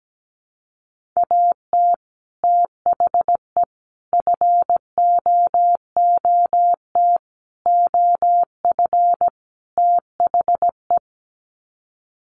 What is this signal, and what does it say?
Morse code, heard as a keyed tone, about 17 wpm, 700 Hz. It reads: AT THE FOOT OF THE
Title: Sets of 5 Words 17wpm